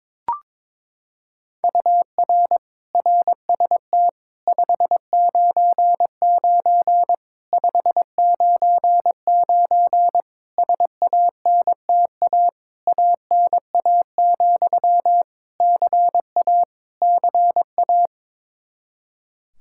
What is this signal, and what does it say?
Morse code, heard as a keyed tone, about 22 wpm, 700 Hz. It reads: UR RST 599 599 SANTA ANA, CA CA
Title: Practice QSO - Exchange 1 - Beginning - 22wpm